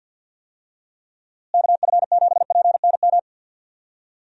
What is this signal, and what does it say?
Morse code, heard as a keyed tone, about 47 wpm, 700 Hz. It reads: KF7PNM
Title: USA Call Signs - 6 Characters - Repeated 2x - 50wpm